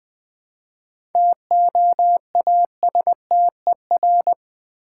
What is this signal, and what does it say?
Morse code, heard as a keyed tone, about 20 wpm, 700 Hz. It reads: TOASTER
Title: Easy Words with Hints - 20wpm